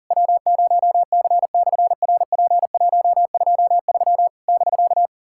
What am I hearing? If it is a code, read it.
Morse code, heard as a keyed tone, about 40 wpm, 700 Hz. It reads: W0C/RP123 <BK>
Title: SOTA QSO Practice 40wpm